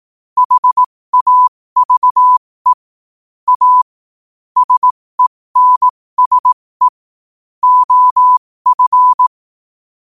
Morse code, heard as a keyed tone, about 18 wpm, spaced as for 15 wpm, 1000 Hz. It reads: HAVE A SENSE OF
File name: qotdfiles / 15wpm